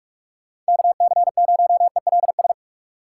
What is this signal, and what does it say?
Morse code, heard as a keyed tone, about 59 wpm, 700 Hz. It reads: KC0ELS